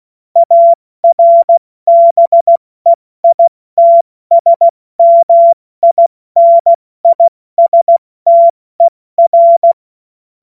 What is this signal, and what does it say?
Morse code, heard as a keyed tone, about 16 wpm, 680 Hz. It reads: ARBEITSMINISTER
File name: Witz-30-September-2025-70-F80